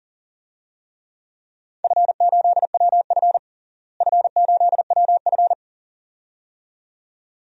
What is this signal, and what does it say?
Morse code, heard as a keyed tone, about 40 wpm, 700 Hz. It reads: F8WF F8WF
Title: DXCC-100 Call Signs - 4 Characters - Repeated 2x - 40wpm